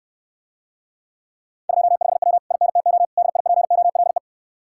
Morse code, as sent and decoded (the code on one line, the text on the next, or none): ..--- ..... .-- / .. -. - --- / -.. .. .--. --- .-.. .
25W INTO DIPOLE